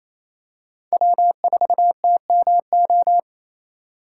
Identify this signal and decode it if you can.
Morse code, heard as a keyed tone, about 28 wpm, 700 Hz. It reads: W4TMO
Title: K1USN SST QSO Practice - 28wpm